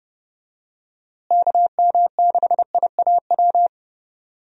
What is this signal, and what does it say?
Morse code, heard as a keyed tone, about 30 wpm, 700 Hz. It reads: KM6IAW
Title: USA Call Signs - 6 Characters - Repeated 2x - 30wpm